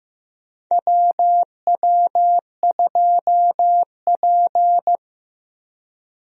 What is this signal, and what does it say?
Morse code, heard as a keyed tone, about 15 wpm, 700 Hz. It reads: WW2P
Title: USA Call Signs - 4 Characters - Repeated 2x - 15wpm